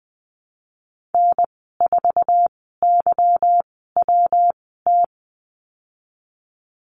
Morse code, heard as a keyed tone, about 19 wpm, spaced as for 14 wpm, 700 Hz. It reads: N4YWT
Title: 161 Lesson 26g - Letter g - Call Signs 20wpm